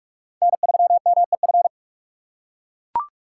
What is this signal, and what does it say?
Morse code, heard as a keyed tone, about 42 wpm, 700 Hz. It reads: N3MEF
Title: USA Call Signs - 5 Characters - 3x Character Spacing - 45wpm